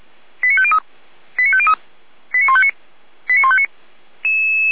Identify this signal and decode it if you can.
Five-tone selective call (ZVEI), heard as quick runs of five tones, tones of about 70 ms. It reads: 89503 89503 89159 89159